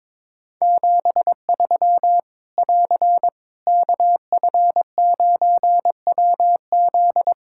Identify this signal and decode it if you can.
Morse code, heard as a keyed tone, about 22 wpm, 700 Hz. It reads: 73 <AR> KF9WZ
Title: Practice QSO - Exchange 4 - Ending - 22wpm